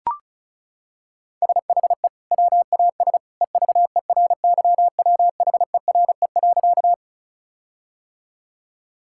Morse code, heard as a keyed tone, about 34 wpm, 700 Hz. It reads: SHE WAS EVERYWHERE.
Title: Sentences from Top 1600 Words 35wpm